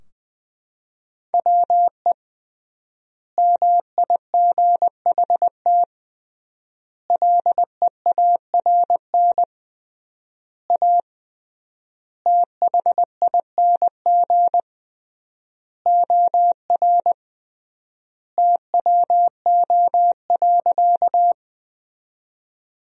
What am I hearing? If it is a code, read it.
Morse code, heard as a keyed tone, about 20 wpm, 700 Hz. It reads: WE MIGHT LEARN A THING OR TWO.
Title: 3x Word-Spacing - Sentences from Top 300 Words - 20wpm